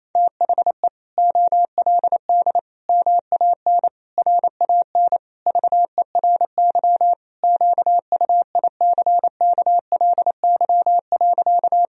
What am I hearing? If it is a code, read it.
Morse code, heard as a keyed tone, about 28 wpm, 700 Hz. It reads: THE OLD MAN RAN VERY QUICKLY.
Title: Sentences from Top 1000 Words - Review Part 3 - 28wpm